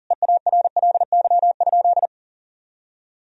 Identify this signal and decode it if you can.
Morse code, heard as a keyed tone, about 40 wpm, 700 Hz. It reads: EARLY?
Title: Sentences from Top 1000 Words - Review Part 2 - 40wpm